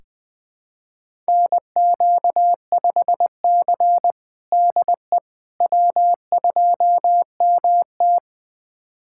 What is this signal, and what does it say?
Morse code, heard as a keyed tone, about 20 wpm, 700 Hz. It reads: NQ5C DE W2MT
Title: USA Callsign DE Callsign - 4 Characters - 20wpm